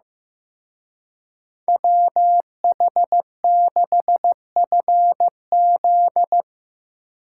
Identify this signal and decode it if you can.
Morse code, heard as a keyed tone, about 17 wpm, 700 Hz. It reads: WH6FZ
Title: Speed Racing - USA 2x2 Call Signs 15wpm